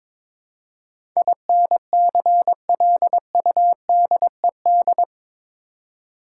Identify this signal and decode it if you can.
Morse code, heard as a keyed tone, about 22 wpm, 700 Hz. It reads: INCLUDED
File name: practice / 22wpm / Eight Letter Words - 22wpm